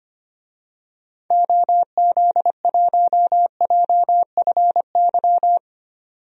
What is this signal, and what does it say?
Morse code, heard as a keyed tone, about 25 wpm, 700 Hz. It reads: OZ1JFY
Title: DXCC-100 Call Signs - 6 Characters - 25wpm